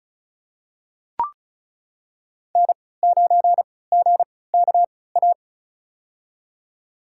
Morse code, heard as a keyed tone, about 34 wpm, spaced as for 20 wpm, 700 Hz. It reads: N9GKA
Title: USA Call Signs - 5 Characters - 3x Character Spacing - 35wpm